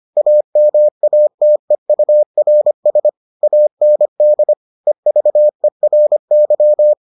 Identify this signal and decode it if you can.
Morse code, heard as a keyed tone, about 25 wpm, 600 Hz. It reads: AMATEURS AND EVERY